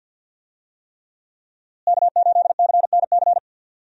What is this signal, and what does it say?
Morse code, heard as a keyed tone, about 51 wpm, 700 Hz. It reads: K8CNC
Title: USA Call Signs - 5 Characters - Repeated 3x - 50wpm